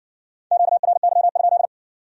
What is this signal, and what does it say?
Morse code, heard as a keyed tone, about 56 wpm, 700 Hz. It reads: QRY?